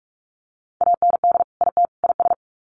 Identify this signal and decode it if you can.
Morse code, heard as a keyed tone, about 44 wpm, 700 Hz. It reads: AND IT IS
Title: Sets of 3 Words 45wpm